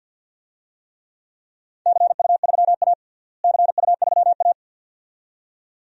Morse code, heard as a keyed tone, about 51 wpm, 700 Hz. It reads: CU3A CU3A
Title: DXCC-100 Call Signs - 4 Characters - Repeated 2x - 50wpm